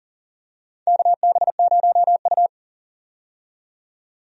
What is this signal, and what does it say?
Morse code, heard as a keyed tone, about 40 wpm, 700 Hz. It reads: KB0U
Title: USA Call Signs - 4 Characters - 40wpm